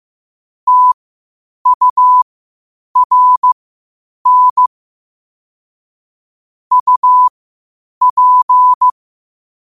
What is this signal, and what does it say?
Morse code, heard as a keyed tone, about 15 wpm, spaced as for 8 wpm, 1000 Hz. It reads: TURN UP
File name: qotdfiles / 8wpm